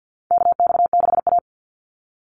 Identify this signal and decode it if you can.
Morse code, heard as a keyed tone, about 50 wpm, 700 Hz. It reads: KX6A
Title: NA SOTA Chaser Callsigns 50wpm